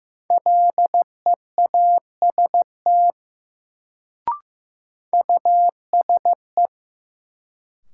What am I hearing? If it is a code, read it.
Morse code, heard as a keyed tone, about 15 wpm, 700 Hz. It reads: LEAST USE
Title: Sets of 2 Words - New Word Drill - 15wpm